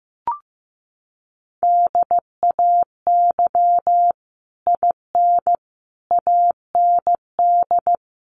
Morse code, heard as a keyed tone, about 15 wpm, 700 Hz. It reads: DAY IN AND
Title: Sets of 5 Words Encore2 15wpm